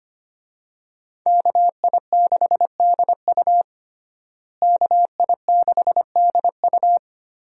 Morse code, heard as a keyed tone, about 25 wpm, 700 Hz. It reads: KI6DU KI6DU
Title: USA Call Signs - 5 Characters - Repeated 2x - 25wpm